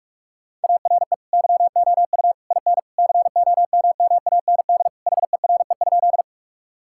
Morse code, heard as a keyed tone, about 41 wpm, 700 Hz. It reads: ARE YOU IN COMMAND HERE?
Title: Sentences from Top 1600 Words 45wpm